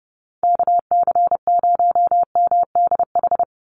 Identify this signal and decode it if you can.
Morse code, heard as a keyed tone, about 29 wpm, 700 Hz. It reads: KC0MDH
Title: USA Call Signs 30wpm